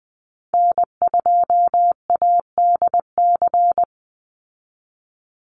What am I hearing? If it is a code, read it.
Morse code, heard as a keyed tone, about 20 wpm, 700 Hz. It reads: N2ADC
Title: Call Signs Repeated 3x - 20wpm